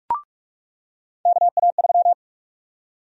Morse code, heard as a keyed tone, about 42 wpm, 700 Hz. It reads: KA3
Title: USA Call Sign Prefixes - 45wpm